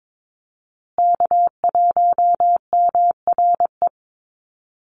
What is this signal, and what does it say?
Morse code, heard as a keyed tone, about 22 wpm, 700 Hz. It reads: K1MRE
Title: USA Call Signs Repeated 2x 22wpm